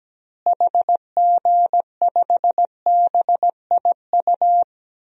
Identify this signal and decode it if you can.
Morse code, heard as a keyed tone, about 17 wpm, 700 Hz. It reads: HG5BIU
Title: DXCC-100 Call Signs - 6 Characters - Repeated 3x - 17wpm